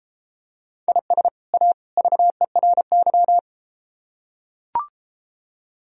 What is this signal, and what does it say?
Morse code, heard as a keyed tone, about 32 wpm, 700 Hz. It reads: IS A VERY